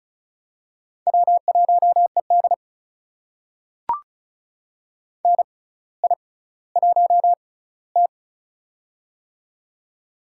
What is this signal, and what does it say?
Morse code, heard as a keyed tone, about 36 wpm, spaced as for 18 wpm, 700 Hz. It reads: W1ED NI1T
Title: USA Call Signs - 4 Characters - 6x Character Spacing - 35wpm